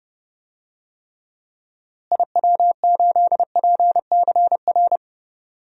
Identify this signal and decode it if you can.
Morse code, heard as a keyed tone, about 30 wpm, 700 Hz. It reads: IW8PCR
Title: DXCC-100 Call Signs - 6 Characters - Repeated 3x - 30wpm